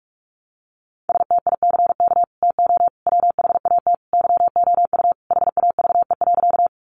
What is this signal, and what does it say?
Morse code, heard as a keyed tone, about 44 wpm, 700 Hz. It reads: STICK TO WHAT YOU HAVE.
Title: Sentences from Top 1200 Words 45wpm